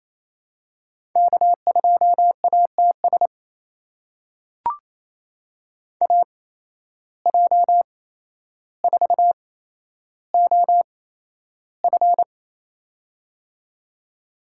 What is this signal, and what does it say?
Morse code, heard as a keyed tone, about 28 wpm, spaced as for 12 wpm, 700 Hz. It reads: K2ATS AJ4OF
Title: USA Call Signs - 5 Characters - 8x Character Spacing - 28wpm